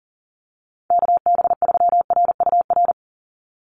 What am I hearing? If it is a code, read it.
Morse code, heard as a keyed tone, about 39 wpm, 700 Hz. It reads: KB3RUR
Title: USA Call Signs 40wpm